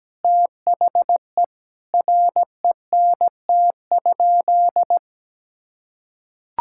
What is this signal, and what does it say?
Morse code, heard as a keyed tone, about 17 wpm, 700 Hz. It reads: THE RENT?
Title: Sentences from Top 1000 Words - Review Part 5 - 17wpm